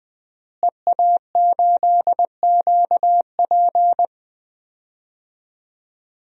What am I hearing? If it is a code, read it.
Morse code, heard as a keyed tone, about 20 wpm, 700 Hz. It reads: EA8QP
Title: DXCC-100 Call Signs - 5 Characters - Repeated 3x - 20wpm